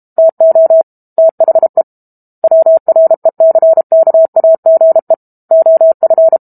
Morse code, heard as a keyed tone, about 32 wpm, 650 Hz. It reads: TO THE WRECKAGE OF
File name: Book Two / 06-Chapter 6 - 32 WPM